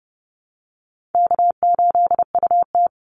Morse code, heard as a keyed tone, about 30 wpm, 700 Hz. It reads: K8UT
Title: Easy Call Signs Repeated 2x - 30wpm